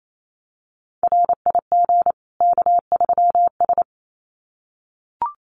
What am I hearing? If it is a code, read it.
Morse code, heard as a keyed tone, about 28 wpm, 700 Hz. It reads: RIG K3S
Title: Common QSO Phrases 28wpm